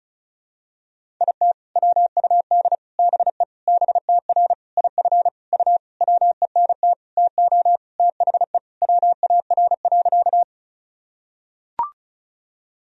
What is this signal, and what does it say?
Morse code, heard as a keyed tone, about 36 wpm, 700 Hz. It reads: IT WUD BE BTR IF U WENT TO THE WAR.
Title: Sentences from Top 400 Words and Abbreviations 35wpm